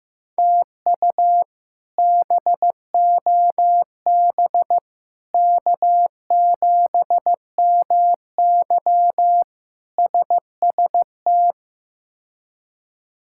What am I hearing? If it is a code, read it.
Morse code, heard as a keyed tone, about 15 wpm, 700 Hz. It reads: TU BOB K7MY SST
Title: K1USN SST QSO Practice - 15wpm